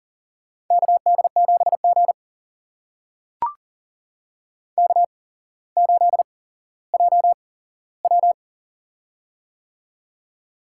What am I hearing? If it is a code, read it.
Morse code, heard as a keyed tone, about 40 wpm, spaced as for 17 wpm, 700 Hz. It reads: KD7G K8JW
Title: USA Call Signs - 8x Character Spacing - 40wpm